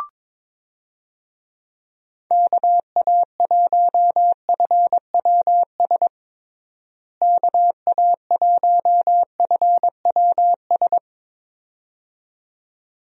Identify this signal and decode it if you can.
Morse code, heard as a keyed tone, about 22 wpm, 700 Hz. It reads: KA1FWS KA1FWS
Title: USA Call Signs - 6 Characters - Repeated 2x - 22wpm